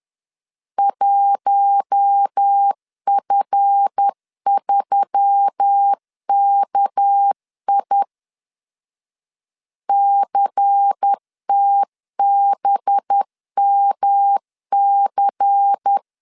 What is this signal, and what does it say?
Morse code, heard as a keyed tone, about 10 wpm, 800 Hz. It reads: 1F3KI CTBMC
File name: LEZIONI / 7 GF9Z / 7 GF 10wpm 2514